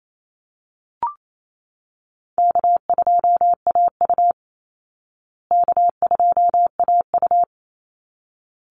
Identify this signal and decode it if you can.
Morse code, heard as a keyed tone, about 28 wpm, 700 Hz. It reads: K2AU K2AU
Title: Call Signs Repeated 3x - 28wpm